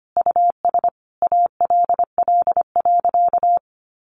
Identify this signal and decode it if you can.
Morse code, heard as a keyed tone, about 25 wpm, 700 Hz. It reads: US ALL.
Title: Sentences from Top 1300 Words 25wpm